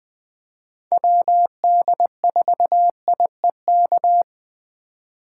Morse code, heard as a keyed tone, about 20 wpm, 700 Hz. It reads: WD4IEK
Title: USA Call Signs - 4x Character Spacing - 20wpm